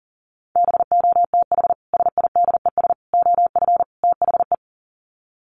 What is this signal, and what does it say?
Morse code, heard as a keyed tone, about 40 wpm, 700 Hz. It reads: BOTH SIDES OF THE
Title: Sets of 4 Words 40wpm